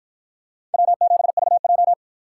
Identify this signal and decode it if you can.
Morse code, heard as a keyed tone, about 52 wpm, 700 Hz. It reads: W7VJ